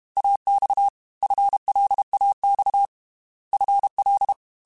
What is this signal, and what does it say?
Morse code, heard as a keyed tone, about 32 wpm, 800 Hz. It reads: AX FLAX FL
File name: FL / FL mp3s / Fast FL @30